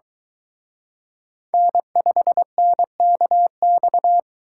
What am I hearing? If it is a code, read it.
Morse code, heard as a keyed tone, about 22 wpm, 700 Hz. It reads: N5NKX